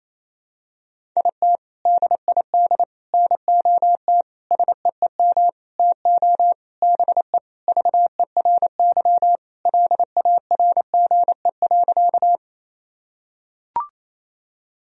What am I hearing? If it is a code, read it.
Morse code, heard as a keyed tone, about 28 wpm, 700 Hz. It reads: IT DID NOT SEEM TO BE VERY LARGE.
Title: Sentences from Top 1000 Words - Review Part 1 - 28wpm